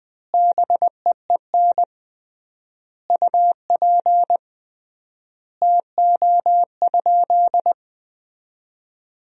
Morse code, heard as a keyed tone, about 20 wpm, 700 Hz. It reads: BEEN UP TO?
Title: 3x Word-Spacing - Sentences from Top 100 Words - 20wpm